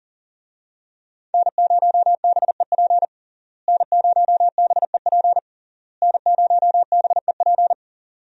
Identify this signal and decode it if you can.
Morse code, heard as a keyed tone, about 40 wpm, 700 Hz. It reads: N0BEP N0BEP N0BEP
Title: USA Call Signs - 5 Characters - Repeated 3x - 40wpm